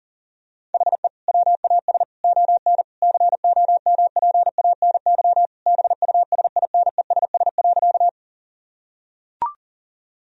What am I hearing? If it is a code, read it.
Morse code, heard as a keyed tone, about 40 wpm, 700 Hz. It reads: HE WAS ON COMPANY BUSINESS.
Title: Sentences from Top 1000 Words - Review Part 3 - 40wpm